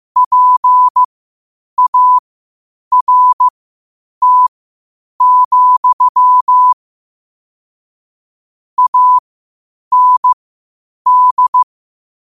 Morse code, heard as a keyed tone, about 15 wpm, spaced as for 8 wpm, 1000 Hz. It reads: PART, AND
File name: qotdfiles / 8wpm